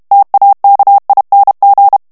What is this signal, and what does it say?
Morse code, heard as a keyed tone, about 32 wpm, 800 Hz. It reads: TAKING
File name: morse mp3 / ANGLAIS texte clair 1 vitesse 30 mots minute